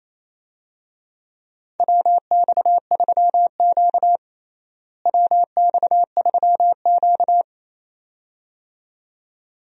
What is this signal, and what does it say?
Morse code, heard as a keyed tone, about 28 wpm, 700 Hz. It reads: WX3Q WX3Q
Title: USA Call Signs - 4 Characters - Repeated 2x - 28wpm